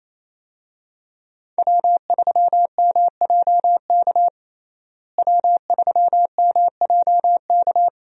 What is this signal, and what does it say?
Morse code, heard as a keyed tone, about 28 wpm, 700 Hz. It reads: W3MJK W3MJK
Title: USA Call Signs - 5 Characters - Repeated 2x - 28wpm